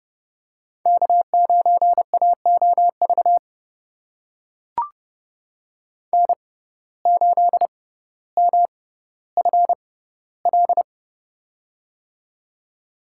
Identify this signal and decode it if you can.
Morse code, heard as a keyed tone, about 30 wpm, spaced as for 15 wpm, 700 Hz. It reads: K9AOV N8MFL
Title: USA Call Signs - 5 Characters - 6x Character Spacing - 30wpm